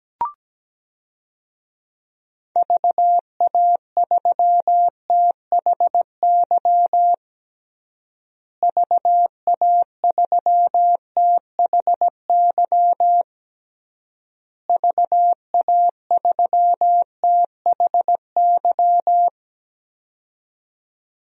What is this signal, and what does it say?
Morse code, heard as a keyed tone, about 17 wpm, 700 Hz. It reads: VA3THY VA3THY VA3THY
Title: DXCC-100 Call Signs - 6 Characters - Repeated 3x - 17wpm